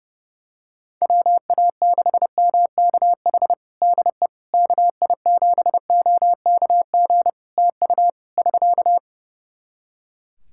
Morse code, heard as a keyed tone, about 30 wpm, 700 Hz. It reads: WA6MKH DE KI7OKG TU <SK>
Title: Practice QSO - Exchange 4 - Ending - 30wpm